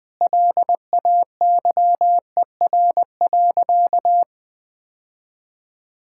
Morse code, heard as a keyed tone, about 20 wpm, 700 Hz. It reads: LAYER.
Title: Sentences from Top 1600 Words 20wpm